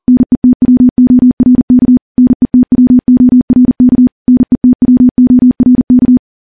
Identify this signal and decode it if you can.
Morse code, heard as a keyed tone, about 39 wpm, 250 Hz. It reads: NETWORK NETWORK NETWORK
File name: morse / morse-training-40WPM